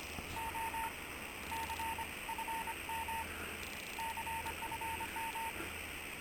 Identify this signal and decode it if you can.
Morse code, heard as a keyed tone, about 25 wpm, spaced as for 19 wpm, 910 Hz, with noise. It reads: OC FM CFM